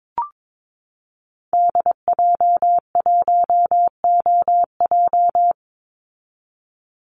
Morse code, heard as a keyed tone, about 22 wpm, 700 Hz. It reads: DJ1OJ
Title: Call Signs Repeated 2x - 22wpm